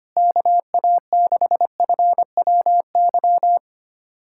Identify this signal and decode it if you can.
Morse code, heard as a keyed tone, about 25 wpm, 700 Hz. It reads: KA6FWY
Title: USA Callsign DE Callsign - 6 Characters - 25wpm